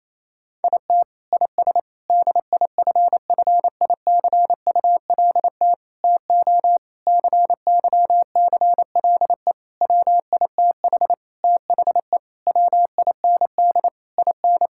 Morse code, heard as a keyed tone, about 28 wpm, 700 Hz. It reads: IT IS DIFFICULT TO CYCLE WITH THE WIND IN
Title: Sentences from Top 1000 Words - Review Part 6 - 28wpm